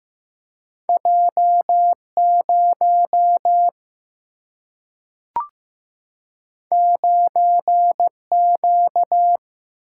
Morse code, heard as a keyed tone, about 15 wpm, 700 Hz. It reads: J 0 9 Q
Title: Single-Letter Single-Number 15wpm